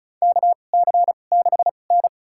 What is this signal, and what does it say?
Morse code, heard as a keyed tone, about 36 wpm, spaced as for 25 wpm, 700 Hz. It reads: KC6N
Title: USA Call Signs - 4 Characters - 2x Character Spacing - 35wpm